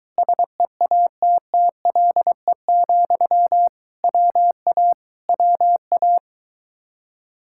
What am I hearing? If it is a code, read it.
Morse code, heard as a keyed tone, about 22 wpm, 700 Hz. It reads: SEATTLE, WA WA